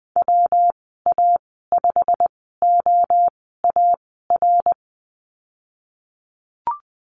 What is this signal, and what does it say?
Morse code, heard as a keyed tone, about 20 wpm, spaced as for 15 wpm, 700 Hz. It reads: WA5OAR
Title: 161 Lesson 26g - Letter g - Call Signs 20wpm